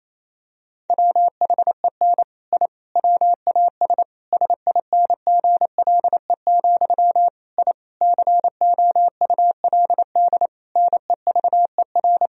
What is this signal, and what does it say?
Morse code, heard as a keyed tone, about 28 wpm, 700 Hz. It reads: WHEN I WAS SINGLE, I COULD NEVER
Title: Sentences from Top 1000 Words - Review Part 4 - 28wpm